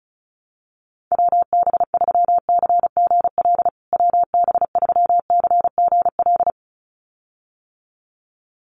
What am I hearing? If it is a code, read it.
Morse code, heard as a keyed tone, about 34 wpm, 700 Hz. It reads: WB3CGL WB3CGL
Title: USA Call Signs Repeated 2x 35wpm